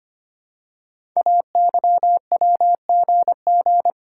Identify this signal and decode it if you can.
Morse code, heard as a keyed tone, about 25 wpm, 700 Hz. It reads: AYWGG
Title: Code Groups - 5 Letters - 25wpm